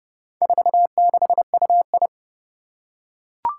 Morse code, heard as a keyed tone, about 30 wpm, 700 Hz. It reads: 46UI
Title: Four Challenging Letter-Number Combinations 30wpm